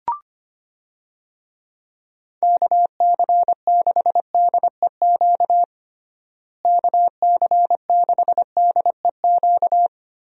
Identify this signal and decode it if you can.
Morse code, heard as a keyed tone, about 25 wpm, 700 Hz. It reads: KC6DEQ KC6DEQ
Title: USA Call Signs - 6 Characters - Repeated 2x - 25wpm